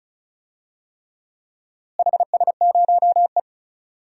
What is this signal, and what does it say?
Morse code, heard as a keyed tone, about 36 wpm, 700 Hz. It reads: HS0E
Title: DXCC-100 Call Signs - 4 Characters - Repeated 3x - 35wpm